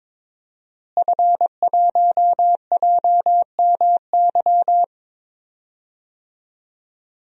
Morse code, heard as a keyed tone, about 22 wpm, 700 Hz. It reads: F1JMY
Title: DXCC-100 Call Signs - 5 Characters - Repeated 2x - 22wpm